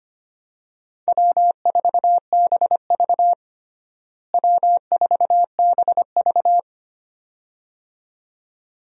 Morse code, heard as a keyed tone, about 25 wpm, 700 Hz. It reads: W4BV W4BV
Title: USA Call Signs - 4 Characters - Repeated 2x - 25wpm